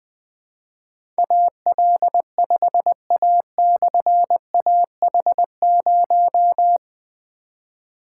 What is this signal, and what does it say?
Morse code, heard as a keyed tone, about 20 wpm, 700 Hz. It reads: AL5A/AH0
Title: Difficult Call Signs 20wpm